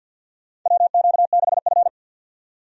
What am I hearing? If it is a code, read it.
Morse code, heard as a keyed tone, about 51 wpm, 700 Hz. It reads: WQ6F
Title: USA Call Signs - 4 Characters - 50wpm